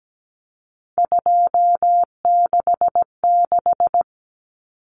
Morse code, heard as a keyed tone, about 17 wpm, 700 Hz. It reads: 266
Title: Common Cut Numbers - 3-Digit 17wpm